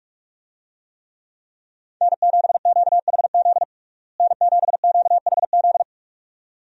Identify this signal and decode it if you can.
Morse code, heard as a keyed tone, about 46 wpm, 700 Hz. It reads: N7QHZ N7QHZ
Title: USA Call Signs - 5 Characters - Repeated 2x - 45wpm